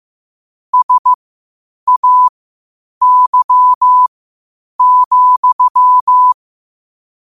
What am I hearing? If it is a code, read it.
Morse code, heard as a keyed tone, about 15 wpm, spaced as for 8 wpm, 1000 Hz. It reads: SAY,
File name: qotdfiles / 8wpm